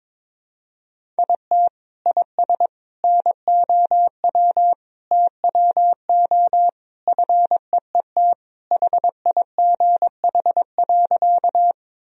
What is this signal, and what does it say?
Morse code, heard as a keyed tone, about 22 wpm, 700 Hz. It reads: IT IS NOW TWO FEET HIGH.
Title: Sentences from Top 1000 Words - Review Part 1 - 22wpm